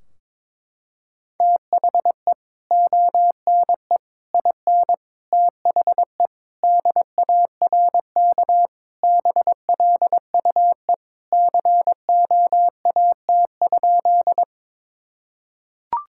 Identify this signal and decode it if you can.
Morse code, heard as a keyed tone, about 22 wpm, 700 Hz. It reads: THE ONE IN THE DARK BLUE COAT?
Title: Sentences from Top 1600 Words 22wpm